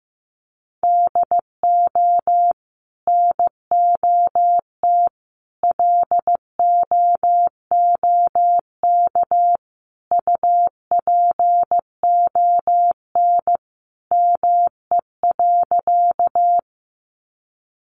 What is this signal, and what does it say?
Morse code, heard as a keyed tone, about 15 wpm, 700 Hz. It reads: DO NOT LOOK UPON ME.
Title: Sentences from Top 1100 Words 15wpm